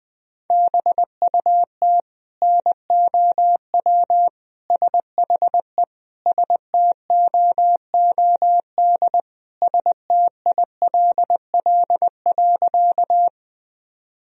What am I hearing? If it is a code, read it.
Morse code, heard as a keyed tone, about 20 wpm, 700 Hz. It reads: BUT NOW SHE STOOD STILL.
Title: Sentences from Top 1000 Words - Review Part 2 - 20wpm